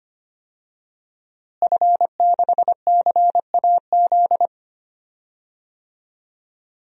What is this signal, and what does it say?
Morse code, heard as a keyed tone, about 25 wpm, 700 Hz. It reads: F6CAZ
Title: DXCC-100 Call Signs - 5 Characters - Repeated 3x - 25wpm